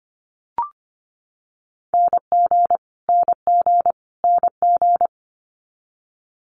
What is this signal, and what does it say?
Morse code, heard as a keyed tone, about 25 wpm, 700 Hz. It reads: NG NG NG
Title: Multi-letter Phonemes Repeated 3x 25wpm